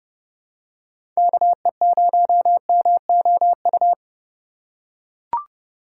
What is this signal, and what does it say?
Morse code, heard as a keyed tone, about 31 wpm, 700 Hz. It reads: KE0MOU
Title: USA Call Signs - 2x Character Spacing - 30wpm